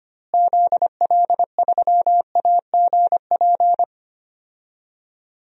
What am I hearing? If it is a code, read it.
Morse code, heard as a keyed tone, about 25 wpm, 700 Hz. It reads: ZL3AGP
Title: DXCC-100 Call Signs - 6 Characters - Repeated 2x - 25wpm